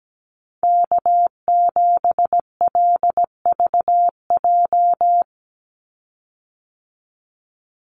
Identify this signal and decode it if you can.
Morse code, heard as a keyed tone, about 17 wpm, 700 Hz. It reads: K7LVJ
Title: Call Signs Repeated 3x - 17wpm